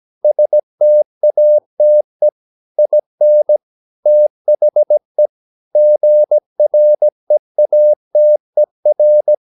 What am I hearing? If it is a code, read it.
Morse code, heard as a keyed tone, about 17 wpm, 600 Hz. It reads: STATE IN THE GREATER
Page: Romanen Never af Ken Follet på engelsk.